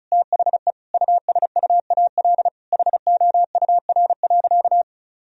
Morse code, heard as a keyed tone, about 36 wpm, 700 Hz. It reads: THE USUAL HOUR.
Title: Sentences from Top 1000 Words - Review Part 2 - 35wpm